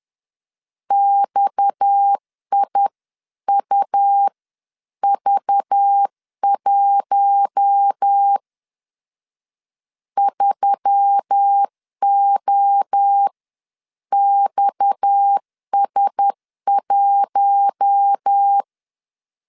Morse code, heard as a keyed tone, about 10 wpm, spaced as for 9 wpm, 800 Hz. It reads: XIUV1 3OXS1
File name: CW / DET / 1 6 1500C S C 10wpm 4252